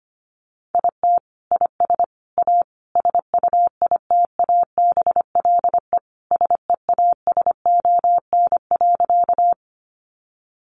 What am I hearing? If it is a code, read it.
Morse code, heard as a keyed tone, about 25 wpm, 700 Hz. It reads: IT IS A SUITABLE SEASON.
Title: Sentences from Top 1400 Words 25wpm